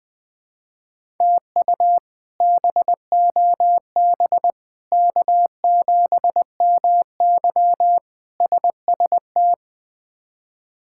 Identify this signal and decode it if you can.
Morse code, heard as a keyed tone, about 20 wpm, 700 Hz. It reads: TU BOB K7MY SST
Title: K1USN SST QSO Practice - 20wpm